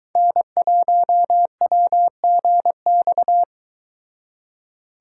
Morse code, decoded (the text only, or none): N1WGX